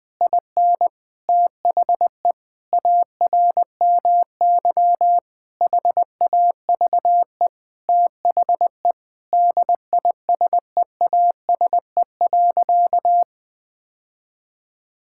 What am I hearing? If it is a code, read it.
Morse code, heard as a keyed tone, about 21 wpm, 700 Hz. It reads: IN THE ARMY HAVE THE DISEASE.
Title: Sentences from Top 1000 Words - Review Part 6 - 20wpm